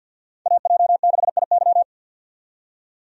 Morse code, heard as a keyed tone, about 52 wpm, 700 Hz. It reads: AJ6IY
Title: USA Call Signs - 5 Characters - 50wpm